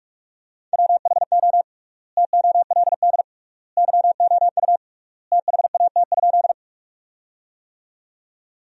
Morse code, heard as a keyed tone, about 44 wpm, 700 Hz. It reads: WHO TOLD YOU THAT?
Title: 3x Word-Spacing - Sentences from Top 400 Words - 45wpm